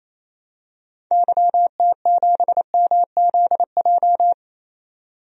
Morse code, heard as a keyed tone, about 28 wpm, 700 Hz. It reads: YT7MZJ
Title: DXCC-100 Call Signs - 6 Characters - 28wpm